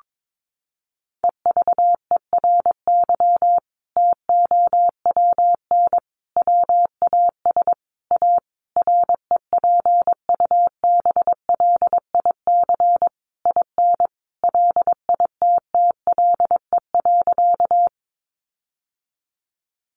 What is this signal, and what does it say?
Morse code, heard as a keyed tone, about 22 wpm, 700 Hz. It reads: EVERY TOWN WAS A REPUBLIC IN LITTLE.
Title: Sentences from Top 1500 Words 22wpm